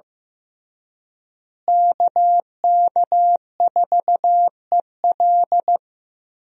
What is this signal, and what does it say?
Morse code, heard as a keyed tone, about 15 wpm, 700 Hz. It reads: KK4EL
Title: USA Call Signs - 5 Characters - Repeated 3x - 15wpm